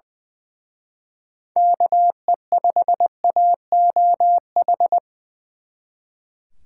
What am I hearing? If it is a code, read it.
Morse code, heard as a keyed tone, about 24 wpm, 700 Hz. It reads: KE5AOH
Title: Speed Racing - USA 2x3 Call Signs 20wpm